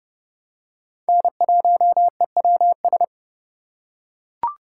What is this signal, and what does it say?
Morse code, heard as a keyed tone, about 30 wpm, 700 Hz. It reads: N1EWS
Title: USA Call Signs - 5 Characters - 4x Character Spacing - 30wpm